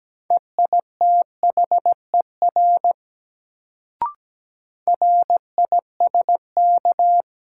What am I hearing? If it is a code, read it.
Morse code, heard as a keyed tone, about 17 wpm, 700 Hz. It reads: EITHER RISK
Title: Sentences from Top 600 Words - New Word Drill - 17wpm